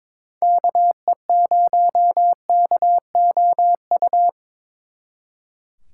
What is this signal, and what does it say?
Morse code, heard as a keyed tone, about 26 wpm, 700 Hz. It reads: KE0KOU
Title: Speed Racing - USA 2x3 Call Signs 22wpm